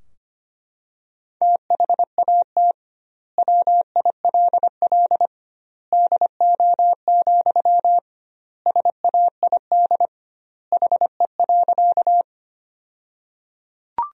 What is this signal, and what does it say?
Morse code, heard as a keyed tone, about 25 wpm, 700 Hz. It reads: THAT WILL DO, SAID HE.
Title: 2x Word-Spacing - Sentences from Top 100 Words - 25wpm